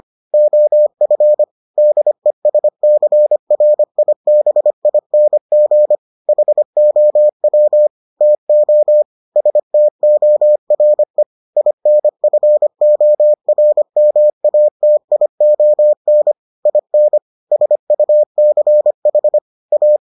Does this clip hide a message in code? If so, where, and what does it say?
Yes: Morse code from 0.3 to 20.0 s, heard as a keyed tone, about 25 wpm, 600 Hz. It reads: OF DESCRIBING HOW TO STORE INFORMATION IN SUCH A